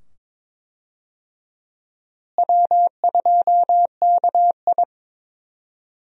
Morse code, heard as a keyed tone, about 21 wpm, 700 Hz. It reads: W2KI
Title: USA Call Signs - 4 Characters - Repeated 2x - 22wpm